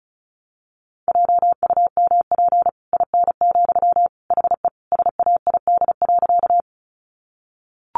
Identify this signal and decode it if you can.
Morse code, heard as a keyed tone, about 34 wpm, 700 Hz. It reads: JUMP IN, HE SAID.
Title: Sentences from Top 1300 Words 35wpm